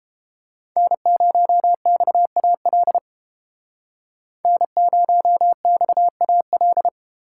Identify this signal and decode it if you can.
Morse code, heard as a keyed tone, about 30 wpm, 700 Hz. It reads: N0XAL N0XAL